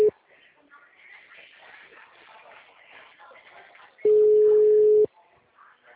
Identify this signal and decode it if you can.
Ringback tone (Europe, 425 Hz), with speech.